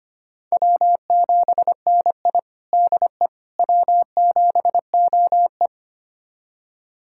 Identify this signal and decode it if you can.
Morse code, heard as a keyed tone, about 25 wpm, 700 Hz. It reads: W7NI DE W7OE
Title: USA Callsign DE Callsign - 4 Characters - 25wpm